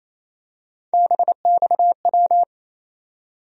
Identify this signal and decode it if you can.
Morse code, heard as a keyed tone, about 28 wpm, 700 Hz. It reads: BXW
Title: USA Call Sign Postfixes - 3 Letters 28wpm